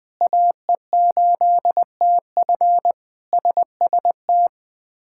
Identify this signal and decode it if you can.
Morse code, heard as a keyed tone, about 20 wpm, 700 Hz. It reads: AE8TF SST
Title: K1USN SST QSO Practice - 20wpm